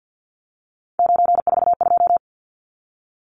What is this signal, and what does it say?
Morse code, heard as a keyed tone, about 50 wpm, 700 Hz. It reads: 942
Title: Numbers - 3-Digit 50wpm